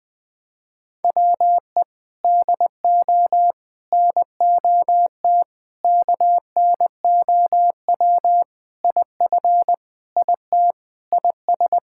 Morse code, heard as a keyed tone, about 21 wpm, 700 Hz. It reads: WE DO NOT KNOW IF IT IS
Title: Sentences with 5-Word Phrases - 20wpm